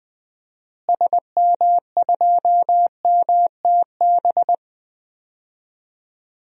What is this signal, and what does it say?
Morse code, heard as a keyed tone, about 20 wpm, 700 Hz. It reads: SM2MTB
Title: DXCC-100 Call Signs - 6 Characters - Repeated 2x - 20wpm